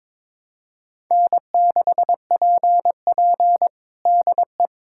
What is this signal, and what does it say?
Morse code, heard as a keyed tone, about 22 wpm, 700 Hz. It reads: N6PP DE
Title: USA Callsign DE Callsign - 4 Characters - 22wpm